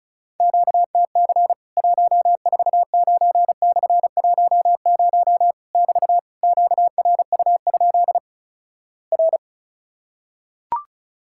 Morse code, heard as a keyed tone, about 36 wpm, 700 Hz. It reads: QTC 149/10 <BT> QRU? R
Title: Worked All Europe DX Contest - QTC Exchange - 35wpm